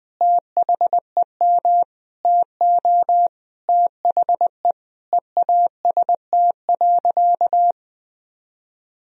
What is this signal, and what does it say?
Morse code, heard as a keyed tone, about 20 wpm, 700 Hz. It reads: THEM TO THE EAST.
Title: Sentences from Top 1000 Words - Review Part 3 - 20wpm